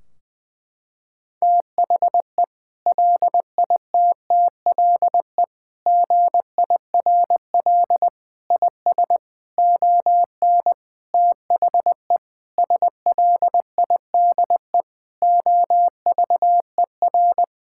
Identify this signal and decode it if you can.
Morse code, heard as a keyed tone, about 21 wpm, 700 Hz. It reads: THE LITTLE GIRL IS ON THE SLIDE OVER
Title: Sentences from Top 1600 Words 20wpm